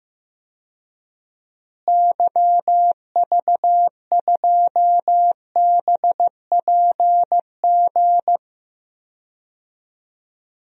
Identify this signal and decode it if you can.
Morse code, heard as a keyed tone, about 15 wpm, 700 Hz. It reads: YV2BPG
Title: DXCC-100 Call Signs - 6 Characters - Repeated 2x - 15wpm